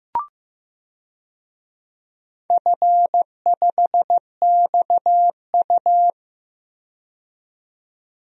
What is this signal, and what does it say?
Morse code, heard as a keyed tone, about 15 wpm, 700 Hz. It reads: F5XU
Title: DXCC-100 Call Signs - 4 Characters - 15wpm